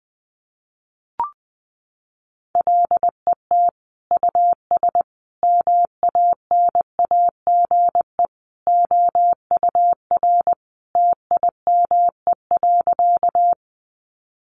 Morse code, heard as a keyed tone, about 20 wpm, 700 Hz. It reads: LET US MANAGE OUR TIME.
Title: Sentences from Top 1400 Words 20wpm